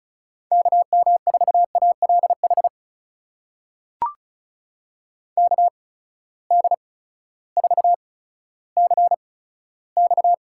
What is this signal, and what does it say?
Morse code, heard as a keyed tone, about 36 wpm, 700 Hz. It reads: K M 4 A L H K D 4 C X
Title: USA Call Signs - 6 Characters - 8x Character Spacing - 35wpm